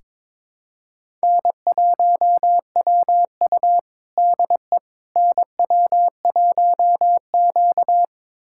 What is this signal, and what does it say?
Morse code, heard as a keyed tone, about 22 wpm, 700 Hz. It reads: N1WU DE NW1Q
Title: USA Callsign DE Callsign - 4 Characters - 22wpm